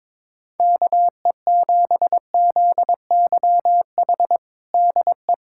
Morse code, heard as a keyed tone, about 21 wpm, 700 Hz. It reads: KE7ZYH DE
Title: USA Callsign DE Callsign - 6 Characters - 22wpm